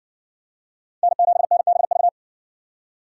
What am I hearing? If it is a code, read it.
Morse code, heard as a keyed tone, about 56 wpm, 700 Hz. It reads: N7NBV